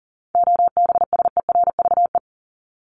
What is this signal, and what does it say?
Morse code, heard as a keyed tone, about 40 wpm, 700 Hz. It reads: OBSERVE
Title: ED Suffix - Top 100 Words with Base Words 40wpm